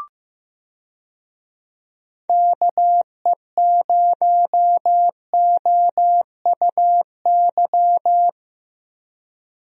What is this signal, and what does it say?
Morse code, heard as a keyed tone, about 16 wpm, 700 Hz. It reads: KE0OUY
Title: USA Call Signs - 6 Characters - Repeated 3x - 15wpm